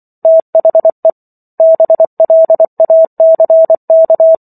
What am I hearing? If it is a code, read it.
Morse code, heard as a keyed tone, about 24 wpm, 650 Hz. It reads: THE BLACK